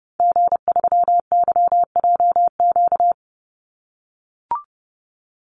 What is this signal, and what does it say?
Morse code, heard as a keyed tone, about 30 wpm, 700 Hz. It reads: G3YJQ
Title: Call Signs - 30wpm